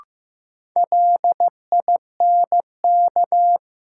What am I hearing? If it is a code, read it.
Morse code, heard as a keyed tone, about 15 wpm, 700 Hz. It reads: LINK
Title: Sentences from Top 1500 Words - New Word Drill Alt - 15wpm